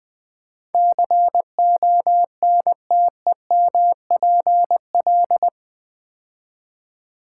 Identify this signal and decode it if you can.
Morse code, heard as a keyed tone, about 20 wpm, 700 Hz. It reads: CONTEMPL
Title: Build A Word - 4 Letters At A Time - 11-Letter Words - 20wpm